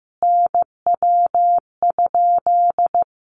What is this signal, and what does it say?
Morse code, heard as a keyed tone, about 15 wpm, 700 Hz. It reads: NW?
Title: Common QSO Phrases 15wpm